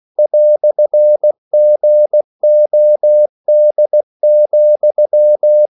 Morse code, heard as a keyed tone, about 16 wpm, 600 Hz. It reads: "GOD,
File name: CW / Practice / EnglishQuotations0216